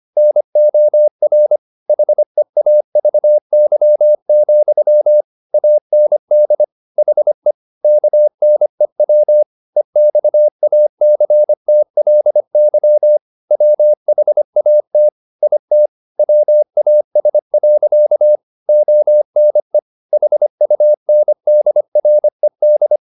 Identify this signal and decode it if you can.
Morse code, heard as a keyed tone, about 25 wpm, 600 Hz. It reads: NOR HEAVY, AND HE KNEW EXACTLY WHAT IT WAS. ONE HUNDRED
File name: the old man and the sea part 1 25wpm